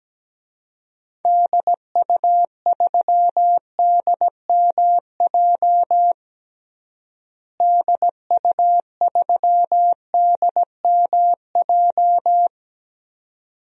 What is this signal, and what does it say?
Morse code, heard as a keyed tone, about 17 wpm, 700 Hz. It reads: DU3DMJ DU3DMJ
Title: DXCC-100 Call Signs - 6 Characters - Repeated 2x - 17wpm